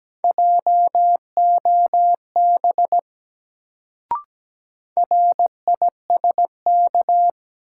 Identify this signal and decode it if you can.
Morse code, heard as a keyed tone, about 17 wpm, 700 Hz. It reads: JOB RISK
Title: Sentences from Top 600 Words - New Word Drill - 17wpm